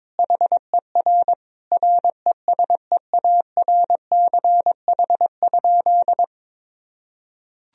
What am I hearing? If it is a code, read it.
Morse code, heard as a keyed tone, about 21 wpm, 700 Hz. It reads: HER RESEARCH?
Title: Sentences from Top 1000 Words - Review Part 4 - 22wpm